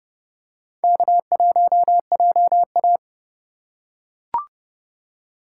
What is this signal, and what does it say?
Morse code, heard as a keyed tone, about 30 wpm, 700 Hz. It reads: K1JA
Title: USA Call Signs - 4 Characters - 3x Character Spacing - 30wpm